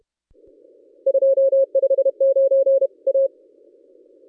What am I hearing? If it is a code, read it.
Morse code, heard as a keyed tone, about 31 wpm, 540 Hz. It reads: 259 A
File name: contests / ss-cw-k5tr-op-n5rz -1200